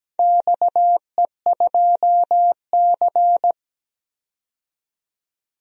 Morse code, heard as a keyed tone, about 17 wpm, 700 Hz. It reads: XE2C
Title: DXCC-100 Call Signs - 4 Characters - 17wpm